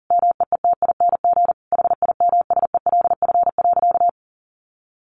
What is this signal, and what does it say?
Morse code, heard as a keyed tone, about 40 wpm, 700 Hz. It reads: MEETING HIMSELF.
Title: A Sound Of Thunder 40wpm